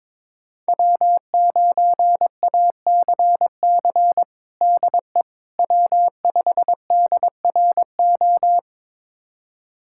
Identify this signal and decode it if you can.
Morse code, heard as a keyed tone, about 22 wpm, 700 Hz. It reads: W9ACC DE W5DRO
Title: USA Callsign DE Callsign - 5 Characters - 22wpm